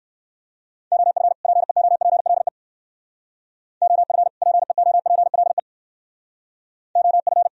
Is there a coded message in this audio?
Yes: Morse code from 0.9 to 7.5 s, heard as a keyed tone, about 57 wpm, 700 Hz. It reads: OF PEOPLE OF PEOPLE OF